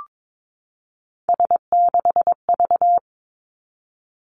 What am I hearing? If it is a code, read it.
Morse code, heard as a keyed tone, about 22 wpm, 700 Hz. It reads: S6V
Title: Three Challenging Letter-Number Combinations 22wpm